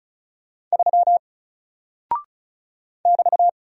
Morse code, heard as a keyed tone, about 34 wpm, 700 Hz. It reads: 3 <BT>
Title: Single Character (Letter:Number:Punctuation:Prosign) Alt 35wpm